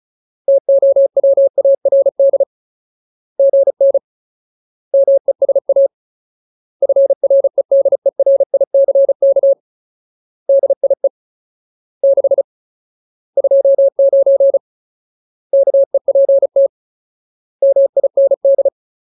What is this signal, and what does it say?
Morse code, heard as a keyed tone, about 34 wpm, 550 Hz. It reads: TOWARD GN MESA FREDERICK DIE 6 29 KEPT MIND
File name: Ultimate Word List / 35WPM Ultimate